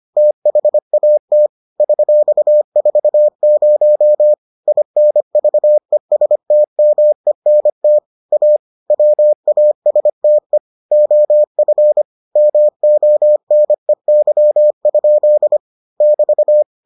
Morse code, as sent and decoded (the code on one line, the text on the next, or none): - .... .- - / ...-..- ....- ----- / .. -. ...- . ... - -- . -. - / .- / .-- .- ... - . / --- ..-. / -- --- -. -.-- ..--.. / -...-
THAT $40 INVESTMENT A WASTE OF MONEY? <BT>